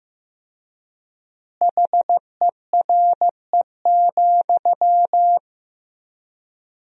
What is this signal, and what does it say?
Morse code, heard as a keyed tone, about 15 wpm, 700 Hz. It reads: HERE,
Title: 3x Word-Spacing - Sentences from Top 300 Words - 15wpm